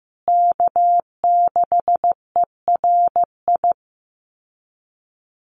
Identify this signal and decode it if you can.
Morse code, heard as a keyed tone, about 15 wpm, 700 Hz. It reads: K6ERI
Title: USA Call Signs Repeated 3x 15wpm